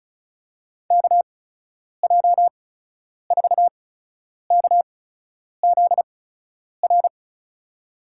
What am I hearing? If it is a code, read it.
Morse code, heard as a keyed tone, about 34 wpm, spaced as for 10 wpm, 700 Hz. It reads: KJ4KZR
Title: USA Call Signs - 6 Characters - 8x Character Spacing - 35wpm